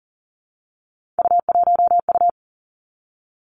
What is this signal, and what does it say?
Morse code, heard as a keyed tone, about 40 wpm, 700 Hz. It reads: U1U
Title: Three Challenging Letter-Number Combinations 40wpm